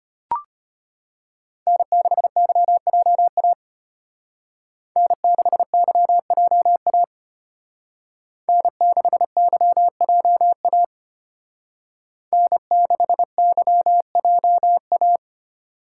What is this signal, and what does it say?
Morse code, heard as a keyed tone, about 32 wpm, 700 Hz. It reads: N6YJA N6YJA N6YJA N6YJA